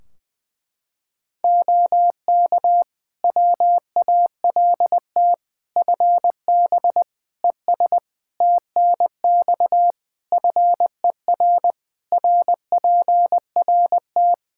Morse code, heard as a keyed tone, about 21 wpm, 700 Hz. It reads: OK WALT FB ES TNX FER RPRT
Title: Practice QSO - Exchange 2 - Info - 20wpm